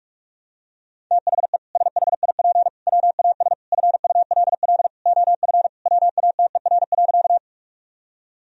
Morse code, heard as a keyed tone, about 41 wpm, 700 Hz. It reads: THE SHIP WAS FULL OF WATER.
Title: Sentences from Top 1000 Words - Review Part 3 - 45wpm